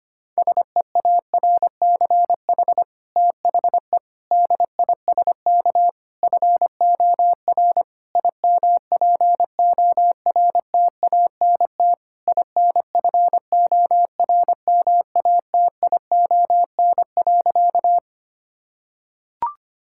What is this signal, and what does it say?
Morse code, heard as a keyed tone, about 25 wpm, 700 Hz. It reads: SEARCH THE DISK FOR IMPORTANT INFORMATION.
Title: Sentences from Top 1000 Words - Review Part 6 - 25wpm